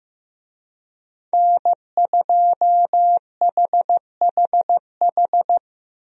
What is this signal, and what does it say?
Morse code, heard as a keyed tone, about 15 wpm, 700 Hz. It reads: N2HHH
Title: USA Call Signs - 5 Characters - Repeated 2x - 15wpm